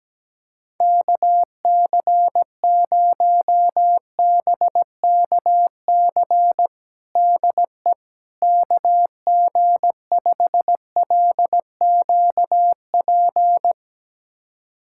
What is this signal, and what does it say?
Morse code, heard as a keyed tone, about 17 wpm, 700 Hz. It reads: KC0BKC DE KG5LQP
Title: USA Callsign DE Callsign - 6 Characters - 17wpm